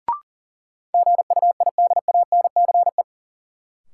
Morse code, heard as a keyed tone, about 40 wpm, 700 Hz. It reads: GUIDANCE
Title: Sentences from Top 1600 Words - New Word Drill - 40wpm